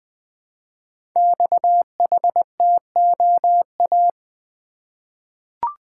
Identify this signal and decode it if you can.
Morse code, heard as a keyed tone, about 20 wpm, 700 Hz. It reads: XHTOA
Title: Code Groups - 5 Letters - 20wpm